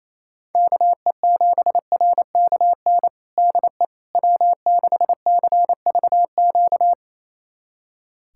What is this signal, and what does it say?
Morse code, heard as a keyed tone, about 28 wpm, 700 Hz. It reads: KE7RKN DE W6CVQ
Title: USA Callsign DE Callsign 28wpm